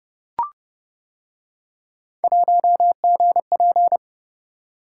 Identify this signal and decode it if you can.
Morse code, heard as a keyed tone, about 29 wpm, 700 Hz. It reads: 1GP